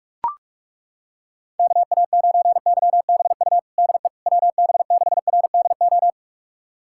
Morse code, heard as a keyed tone, about 41 wpm, 700 Hz. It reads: KA9YBU DE WB6RDO
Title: USA Callsign DE Callsign 45wpm